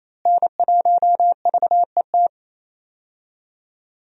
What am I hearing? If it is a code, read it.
Morse code, heard as a keyed tone, about 28 wpm, 700 Hz. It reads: N1VET
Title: USA Call Signs - 5 Characters - Repeated 3x - 28wpm